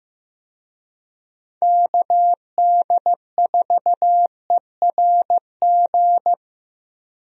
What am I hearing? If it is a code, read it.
Morse code, heard as a keyed tone, about 15 wpm, 700 Hz. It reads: KD4ERG
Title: USA Call Signs - 6 Characters - Repeated 2x - 15wpm